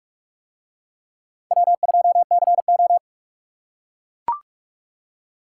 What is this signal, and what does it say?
Morse code, heard as a keyed tone, about 39 wpm, 700 Hz. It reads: W2CO